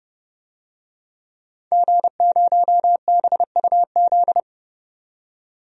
Morse code, heard as a keyed tone, about 30 wpm, 700 Hz. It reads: G0BUZ
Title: DXCC-100 Call Signs - 5 Characters - 30wpm